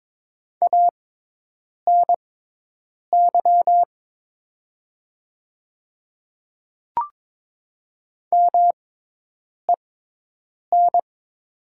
Morse code, heard as a keyed tone, about 22 wpm, spaced as for 8 wpm, 700 Hz. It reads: ANY MEN
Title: 6x Character Spacing - Top 200 Words - 22wpm